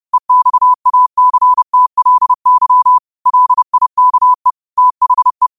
Morse code, heard as a keyed tone, about 30 wpm, 1000 Hz. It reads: EXACTLY LIKE THE